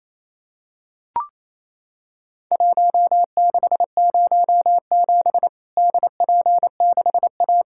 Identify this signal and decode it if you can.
Morse code, heard as a keyed tone, about 28 wpm, 700 Hz. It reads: 1607 DP6A
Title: Worked All Europe DX Contest - QTC Exchange - 28wpm